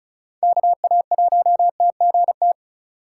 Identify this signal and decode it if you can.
Morse code, heard as a keyed tone, about 36 wpm, 700 Hz. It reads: KA1TGT
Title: USA Call Signs - 6 Characters - 2x Character Spacing - 35wpm